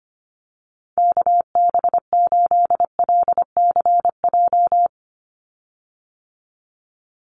Morse code, heard as a keyed tone, about 25 wpm, 700 Hz. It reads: KB8LCJ
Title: USA Call Signs 25wpm